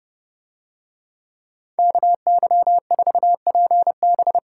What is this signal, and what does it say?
Morse code, heard as a keyed tone, about 28 wpm, 700 Hz. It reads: KY4PB